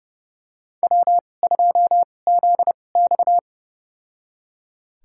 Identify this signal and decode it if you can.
Morse code, heard as a keyed tone, about 30 wpm, spaced as for 22 wpm, 700 Hz. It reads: W2ZX
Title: USA Call Signs - 2x Character Spacing - 30wpm